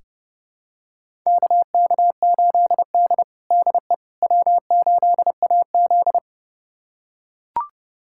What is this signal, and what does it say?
Morse code, heard as a keyed tone, about 32 wpm, 700 Hz. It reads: KK8D DE W8AZ
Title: USA Callsign DE Callsign - 4 Characters - 30wpm